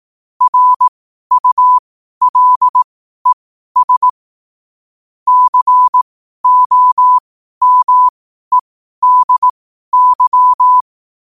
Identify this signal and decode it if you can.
Morse code, heard as a keyed tone, about 18 wpm, spaced as for 12 wpm, 1000 Hz. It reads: RULES COMEDY
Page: qotd-at-12-wpm